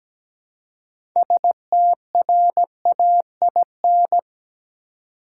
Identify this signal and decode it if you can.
Morse code, heard as a keyed tone, about 17 wpm, 700 Hz. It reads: STRAIN
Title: Six Letter Words - 17wpm